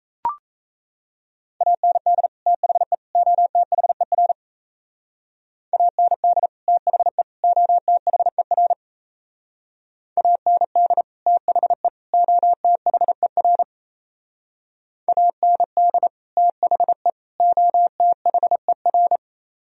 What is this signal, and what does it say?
Morse code, heard as a keyed tone, about 36 wpm, 700 Hz. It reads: AND THE OTHER AND THE OTHER AND THE OTHER AND THE OTHER